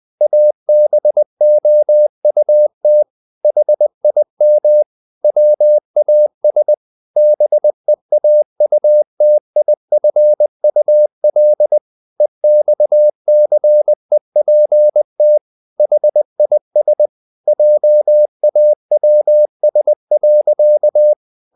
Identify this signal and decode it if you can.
Morse code, heard as a keyed tone, about 20 wpm, 600 Hz. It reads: ABOUT HIM WAS BEAUTIFUL EXCEPT HIS JAWS.
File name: the old man and the sea part 2 20wpm